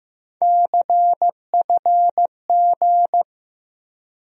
Morse code, heard as a keyed tone, about 15 wpm, 700 Hz. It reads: CFG
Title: USA Call Sign Postfixes - 3 Letters 15wpm